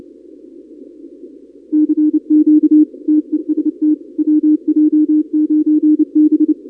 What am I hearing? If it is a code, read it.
Morse code, heard as a keyed tone, about 30 wpm, 310 Hz. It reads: CQ TEST WJ9B
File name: contests / 2019-cw-wpx-kt5j-k5pi- -2330 0-31